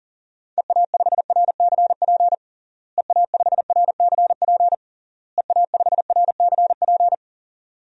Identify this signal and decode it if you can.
Morse code, heard as a keyed tone, about 40 wpm, 700 Hz. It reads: EA5RCP EA5RCP EA5RCP
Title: DXCC-100 Call Signs - 6 Characters - Repeated 3x - 40wpm